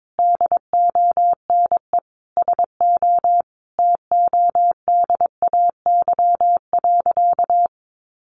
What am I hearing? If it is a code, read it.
Morse code, heard as a keyed tone, about 22 wpm, 700 Hz. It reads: DONE SO TODAY.
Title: Sentences from Top 1500 Words 22wpm